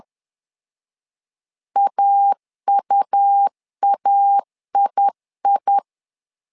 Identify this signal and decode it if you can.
Morse code, heard as a keyed tone, about 10 wpm, 800 Hz. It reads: AUAII
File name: LEZIONI / 2 UHDN / 2 UH 10wpm 1714